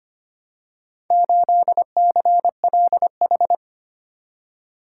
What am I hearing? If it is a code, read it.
Morse code, heard as a keyed tone, about 25 wpm, 700 Hz. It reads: 8CLH
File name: practice / 25wpm / Four Letter-Number Combinations - 25wpm